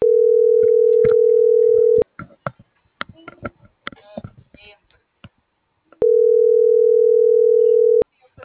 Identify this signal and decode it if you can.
Ringback tone (North America), 440 + 480 Hz.